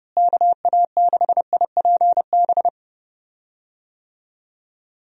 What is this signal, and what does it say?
Morse code, heard as a keyed tone, about 30 wpm, 700 Hz. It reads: KA6IPB
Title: USA Call Signs - 6 Characters - Repeated 2x - 30wpm